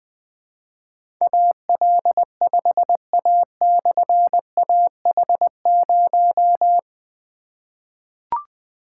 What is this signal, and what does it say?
Morse code, heard as a keyed tone, about 20 wpm, 700 Hz. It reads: AL5A/AH0
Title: Difficult Call Signs 20wpm